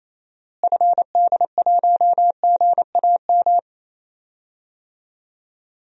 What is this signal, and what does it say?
Morse code, heard as a keyed tone, about 28 wpm, 700 Hz. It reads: FD1GAM
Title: DXCC-100 Call Signs - 6 Characters - Repeated 2x - 28wpm